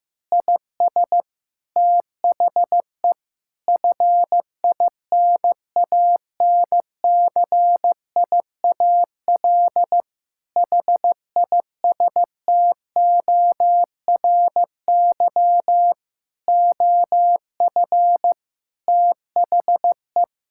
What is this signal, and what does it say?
Morse code, heard as a keyed tone, about 15 wpm, 700 Hz. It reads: IS THE FINANCIAL HISTORY OF THE
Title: Sentences from Top 1000 Words - Review Part 5 - 15wpm